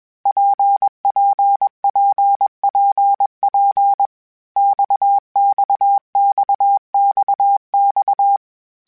Morse code, heard as a keyed tone, about 21 wpm, 800 Hz. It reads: PPPPP XXXXX
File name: Groups@20 / Groups@20 mp3s / RKPX@20